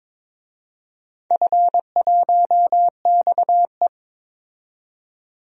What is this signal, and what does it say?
Morse code, heard as a keyed tone, about 22 wpm, 700 Hz. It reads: F1XE
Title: DXCC-100 Call Signs - 4 Characters - 22wpm